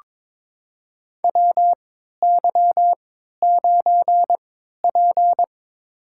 Morse code, heard as a keyed tone, about 22 wpm, spaced as for 13 wpm, 700 Hz. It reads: WY9P
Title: USA Call Signs - 4 Characters - 3x Character Spacing - 22wpm